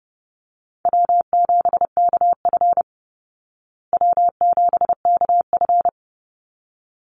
Morse code, heard as a keyed tone, about 30 wpm, 700 Hz. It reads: W7KF W7KF
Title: Call Signs Repeated 3x - 30wpm